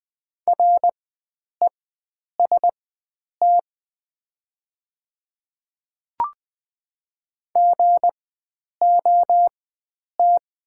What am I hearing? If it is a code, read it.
Morse code, heard as a keyed tone, about 20 wpm, spaced as for 9 wpm, 700 Hz. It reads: REST GOT
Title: 4x Character Spacing - Top 500 Words - 20wpm